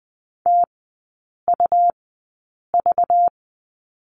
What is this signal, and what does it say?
Morse code, heard as a keyed tone, about 20 wpm, spaced as for 8 wpm, 700 Hz. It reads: TUV
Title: 191 Lesson 31e - Letter v - Three Characters 20wpm